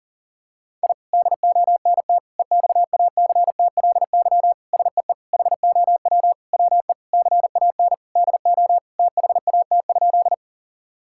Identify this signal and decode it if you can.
Morse code, heard as a keyed tone, about 40 wpm, 700 Hz. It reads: I DONT EXACTLY SEE HOW WE CAN DO THAT?
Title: Sentences from Top 1000 Words - Review Part 6 - 40wpm